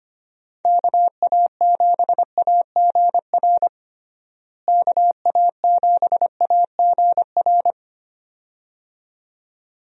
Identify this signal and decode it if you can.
Morse code, heard as a keyed tone, about 25 wpm, 700 Hz. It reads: KA7AGR KA7AGR
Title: USA Call Signs - 6 Characters - Repeated 2x - 25wpm